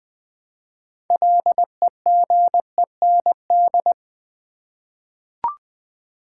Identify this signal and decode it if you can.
Morse code, heard as a keyed tone, about 20 wpm, 700 Hz. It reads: LEGEND
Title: 6 Letter Words - First 3 Letters Given - 20wpm